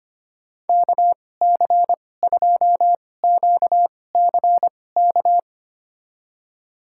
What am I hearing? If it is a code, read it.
Morse code, heard as a keyed tone, about 25 wpm, spaced as for 18 wpm, 700 Hz. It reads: KC2QCK
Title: USA Call Signs - 6 Characters - 2x Character Spacing - 25wpm